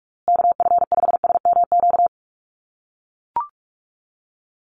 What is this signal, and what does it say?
Morse code, heard as a keyed tone, about 44 wpm, 700 Hz. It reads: KF5SMQ
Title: USA Call Signs Repeated 3x 45wpm